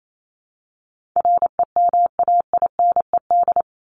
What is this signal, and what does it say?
Morse code, heard as a keyed tone, about 28 wpm, 700 Hz. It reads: REMAINED
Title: ED Suffix - Top 100 Words with Base Words 28wpm